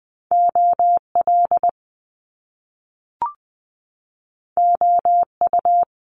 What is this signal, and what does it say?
Morse code, heard as a keyed tone, about 20 wpm, 700 Hz. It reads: O L O U
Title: Top Two Letter Combinations in Words 20wpm